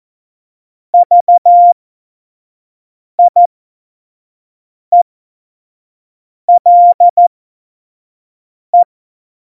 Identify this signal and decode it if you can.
Morse code, heard as a keyed tone, about 14 wpm, spaced as for 5 wpm, 700 Hz. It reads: VIELE
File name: Morsekurs / Witze / Witz 13. Oktober 2020, 25 F70